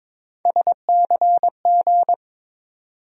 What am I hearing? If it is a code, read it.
Morse code, heard as a keyed tone, about 21 wpm, 700 Hz. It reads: SCG
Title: USA Call Sign Postfixes - 3 Letters 22wpm